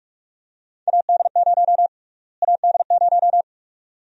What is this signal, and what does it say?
Morse code, heard as a keyed tone, about 46 wpm, 690 Hz. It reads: AD0 AD0
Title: USA Call Sign Prefixes - Repeated 2x - 45wpm